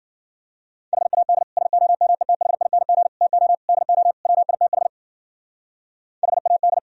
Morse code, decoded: HAD SOMETHING TO DO WITH HAD

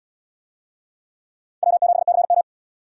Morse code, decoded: W7CM

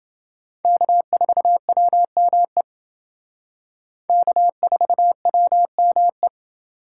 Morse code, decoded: K4WME K4WME

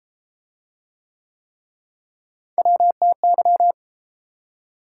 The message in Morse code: WTY

